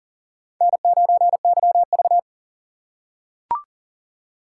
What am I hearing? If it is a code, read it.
Morse code, heard as a keyed tone, about 40 wpm, 700 Hz. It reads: N9YV
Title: USA Call Signs - 4 Characters - 3x Character Spacing - 40wpm